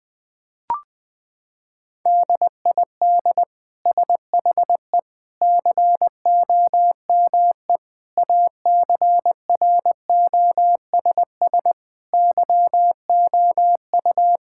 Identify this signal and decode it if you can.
Morse code, heard as a keyed tone, about 20 wpm, 700 Hz. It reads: DID SHE COME ACROSS YOU
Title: Sentences from Top 1000 Words - Review Part 5 - 20wpm